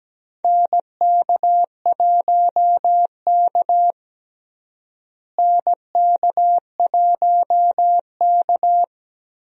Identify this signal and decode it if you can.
Morse code, heard as a keyed tone, about 17 wpm, 700 Hz. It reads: NK1K NK1K
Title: USA Call Signs - 4 Characters - Repeated 2x - 17wpm